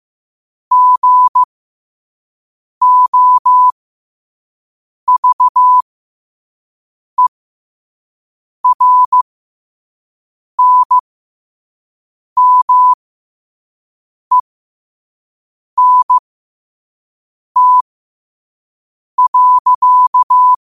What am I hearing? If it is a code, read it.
Morse code, heard as a keyed tone, about 15 wpm, spaced as for 5 wpm, 1000 Hz. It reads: GOVERNMENT.
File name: qotdfiles / 5wpm